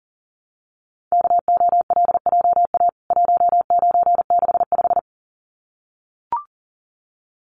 Morse code, heard as a keyed tone, about 40 wpm, 700 Hz. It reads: KOLJA 1965
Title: CWT Exchange 40wpm